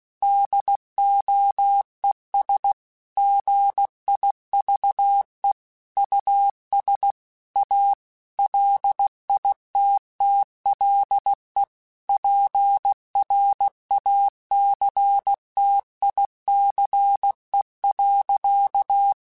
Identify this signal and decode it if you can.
Morse code, heard as a keyed tone, about 16 wpm, 800 Hz. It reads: DOES GIVE US A LITTLE PRACTICE.
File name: QSOs / QSOs-mp3s / WD0BGP 2@15